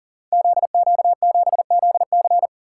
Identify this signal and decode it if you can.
Morse code, heard as a keyed tone, about 40 wpm, 700 Hz. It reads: ZQ7ZC
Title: Code Groups - 5 Letter:Numbers - 40wpm